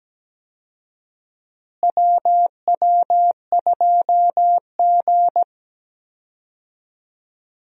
Morse code, heard as a keyed tone, about 17 wpm, 700 Hz. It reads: WW2G
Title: USA Call Signs - 4 Characters - Repeated 2x - 17wpm